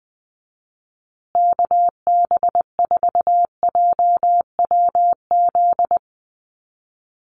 Morse code, heard as a keyed tone, about 19 wpm, 700 Hz. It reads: KB4JWZ
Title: USA Call Signs Repeated 2x 20wpm